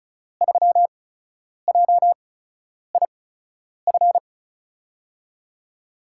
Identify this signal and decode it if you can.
Morse code, heard as a keyed tone, about 36 wpm, spaced as for 10 wpm, 700 Hz. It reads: 3JIF
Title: USA Call Signs - 6 Characters - 8x Character Spacing - 35wpm